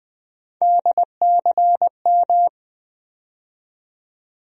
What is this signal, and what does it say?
Morse code, heard as a keyed tone, about 19 wpm, 700 Hz. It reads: DCM
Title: USA Call Sign Postfixes - 3 Letters 20wpm